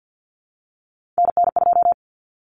Morse code, heard as a keyed tone, about 48 wpm, 700 Hz. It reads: NN2
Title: Common Cut Numbers - 3-Digit 50wpm